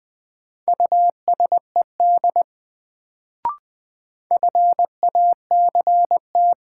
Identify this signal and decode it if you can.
Morse code, heard as a keyed tone, about 20 wpm, 700 Hz. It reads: USED FACT
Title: Sets of 2 Words - New Word Drill - 20wpm